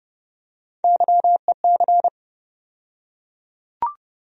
Morse code, heard as a keyed tone, about 30 wpm, 700 Hz. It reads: YEC
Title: USA Call Sign Postfixes - 3 Letters 30wpm